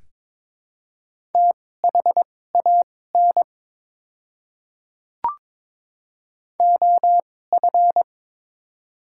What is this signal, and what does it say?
Morse code, heard as a keyed tone, about 22 wpm, spaced as for 16 wpm, 700 Hz. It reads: THAN OF
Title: 2x Character Spacing - Top 100 Words - 22wpm